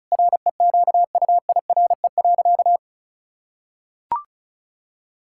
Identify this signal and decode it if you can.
Morse code, heard as a keyed tone, about 36 wpm, 700 Hz. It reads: REQUIRE.
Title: Sentences from Top 1000 Words - Review Part 6 - 35wpm